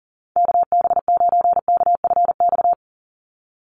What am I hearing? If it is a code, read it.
Morse code, heard as a keyed tone, about 40 wpm, 700 Hz. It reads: KB9KFX
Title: USA Call Signs 40wpm